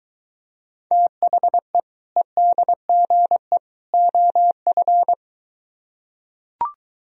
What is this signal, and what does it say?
Morse code, heard as a keyed tone, about 23 wpm, 700 Hz. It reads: THE EDGE OF